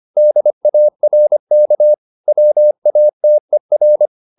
Morse code, heard as a keyed tone, about 25 wpm, 600 Hz. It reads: DARK WATER
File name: the old man and the sea part 1 25wpm